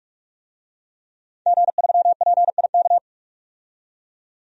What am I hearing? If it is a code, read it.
Morse code, heard as a keyed tone, about 46 wpm, 700 Hz. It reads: G3PIK
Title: DXCC-100 Call Signs - 5 Characters - 45wpm